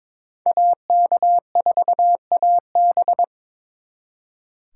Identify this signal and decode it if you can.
Morse code, heard as a keyed tone, about 26 wpm, 700 Hz. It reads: AK4AB
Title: Speed Racing - USA Call Signs 22wpm